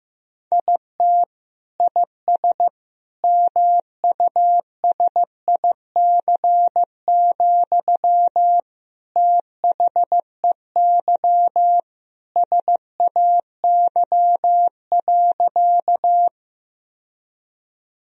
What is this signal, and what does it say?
Morse code, heard as a keyed tone, about 15 wpm, 700 Hz. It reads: IT IS MUSIC, THEY SAY.
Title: Sentences from Top 1000 Words - Review Part 1 - 15wpm